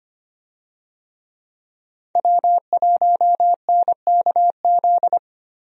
Morse code, heard as a keyed tone, about 25 wpm, 700 Hz. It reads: W1NKZ
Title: USA Call Signs - 5 Characters - Repeated 3x - 25wpm